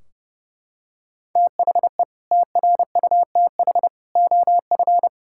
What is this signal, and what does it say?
Morse code, heard as a keyed tone, about 30 wpm, 700 Hz. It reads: THE TRUTH OF
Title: Sentences from Top 1600 Words 30wpm